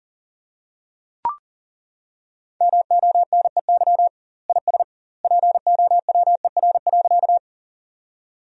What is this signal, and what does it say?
Morse code, heard as a keyed tone, about 40 wpm, 700 Hz. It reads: MONEY IS POWER.
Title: 2x Word-Spacing - Sentences from Top 500 Words - 40wpm